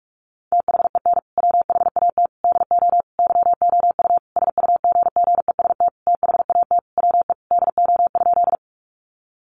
Morse code, heard as a keyed tone, about 44 wpm, 700 Hz. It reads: THEN WHAT DO YOU SUGGEST THAT WE DO?
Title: Sentences from Top 1300 Words 45wpm